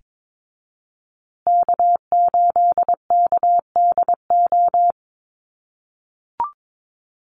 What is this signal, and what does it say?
Morse code, heard as a keyed tone, about 22 wpm, 700 Hz. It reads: K8KDO
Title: USA Call Signs Repeated 2x 22wpm